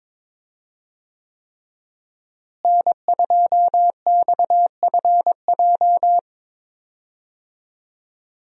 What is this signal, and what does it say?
Morse code, heard as a keyed tone, about 22 wpm, 700 Hz. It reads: N2XFJ
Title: USA Call Signs - 5 Characters - Repeated 3x - 22wpm